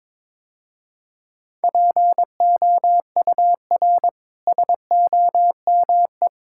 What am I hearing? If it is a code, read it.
Morse code, heard as a keyed tone, about 22 wpm, 700 Hz. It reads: POUR SOME
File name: practice / 22wpm / Easy Sentences with Hints - 22wpm